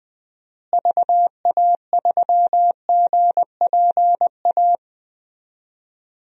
Morse code, heard as a keyed tone, about 20 wpm, 700 Hz. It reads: VA3GPA
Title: DXCC-100 Call Signs - 6 Characters - Repeated 2x - 20wpm